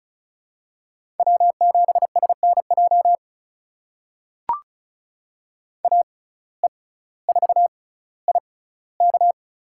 Morse code, heard as a keyed tone, about 36 wpm, spaced as for 18 wpm, 700 Hz. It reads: W7SNJ AE4IK
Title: USA Call Signs - 5 Characters - 6x Character Spacing - 35wpm